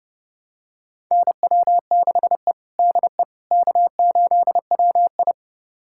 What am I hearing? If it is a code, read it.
Morse code, heard as a keyed tone, about 32 wpm, 700 Hz. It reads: NW6E DE K8WI
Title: USA Callsign DE Callsign - 4 Characters - 30wpm